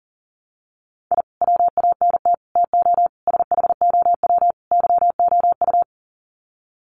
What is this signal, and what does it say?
Morse code, heard as a keyed tone, about 40 wpm, 700 Hz. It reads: I WANT TO SHOW YOU
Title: Sets of 5 Words 40wpm